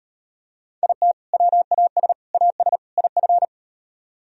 Morse code, heard as a keyed tone, about 38 wpm, 700 Hz. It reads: IT WAS AS IF